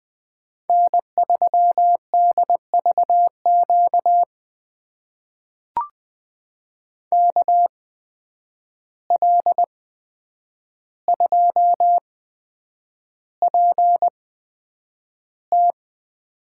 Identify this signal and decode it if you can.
Morse code, heard as a keyed tone, about 20 wpm, spaced as for 9 wpm, 700 Hz. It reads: N3DVQ KL2PT
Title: USA Call Signs - 5 Characters - 8x Character Spacing - 20wpm